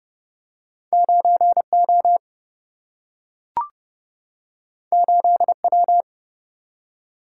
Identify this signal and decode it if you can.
Morse code, heard as a keyed tone, about 30 wpm, 700 Hz. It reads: 9 O 8 W
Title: Single-Letter Single-Number 30wpm